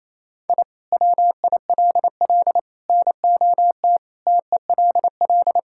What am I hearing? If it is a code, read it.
Morse code, heard as a keyed tone, about 28 wpm, 700 Hz. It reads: I WILL NOT TELL
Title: Sentences from Top 1000 Words - Review Part 5 - 28wpm